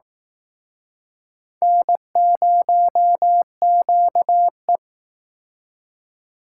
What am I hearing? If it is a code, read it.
Morse code, heard as a keyed tone, about 18 wpm, 700 Hz. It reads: N0QE
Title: Speed Racing - USA 1x2 Call Signs 15wpm